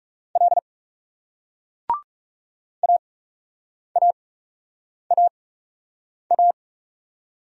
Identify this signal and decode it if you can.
Morse code, heard as a keyed tone, about 39 wpm, spaced as for 8 wpm, 700 Hz. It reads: L AAAA